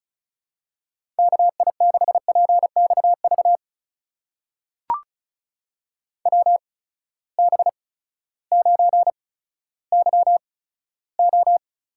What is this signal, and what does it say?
Morse code, heard as a keyed tone, about 36 wpm, spaced as for 16 wpm, 700 Hz. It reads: KI6PXV WB9YO
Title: USA Call Signs - 6 Characters - 8x Character Spacing - 35wpm